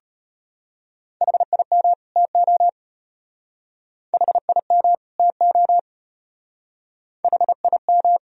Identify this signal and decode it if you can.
Morse code, heard as a keyed tone, about 30 wpm, 700 Hz. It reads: HIM TO HIM TO HIM
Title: Speed-Racing - Sets of 2 Words 25wpm